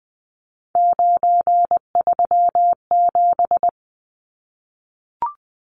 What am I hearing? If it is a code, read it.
Morse code, heard as a keyed tone, about 20 wpm, 700 Hz. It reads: 937
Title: Numbers - 3-Digit 20wpm